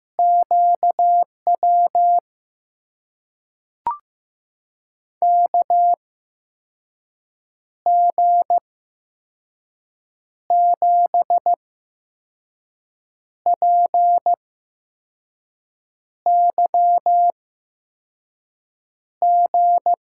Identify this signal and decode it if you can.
Morse code, heard as a keyed tone, about 15 wpm, spaced as for 5 wpm, 700 Hz. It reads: QW KG7PYG
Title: USA Call Signs - 6 Characters - 8x Character Spacing - 15wpm